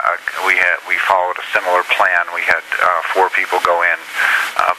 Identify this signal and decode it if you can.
Emergency Alert System (SAME) tones, heard as harsh buzzing bursts, with speech. It is heard from 0.4 to 4.5 s.